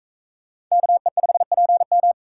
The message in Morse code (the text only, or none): KE5PM